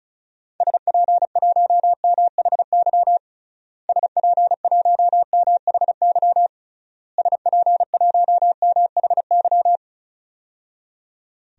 Morse code, heard as a keyed tone, about 36 wpm, 700 Hz. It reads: SP1MHY SP1MHY SP1MHY
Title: DXCC-100 Call Signs - 6 Characters - Repeated 3x - 35wpm